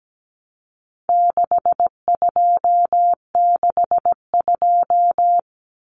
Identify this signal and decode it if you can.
Morse code, heard as a keyed tone, about 17 wpm, 700 Hz. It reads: 6262
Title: Numbers - 4-Digit 17wpm